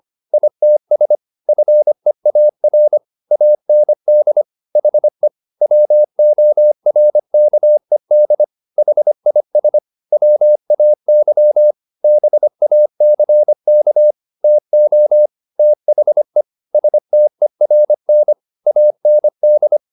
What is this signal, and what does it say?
Morse code, heard as a keyed tone, about 25 wpm, 600 Hz. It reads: ITS FEAR AND HE WORKED HIS WAY BACK TO THE STERN AND